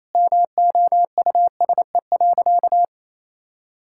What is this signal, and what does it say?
Morse code, heard as a keyed tone, about 28 wpm, 700 Hz. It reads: MOUSE.
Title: Sentences from Top 1000 Words - Review Part 5 - 28wpm